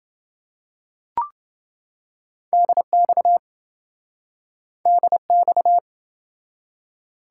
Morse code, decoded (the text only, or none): DX DX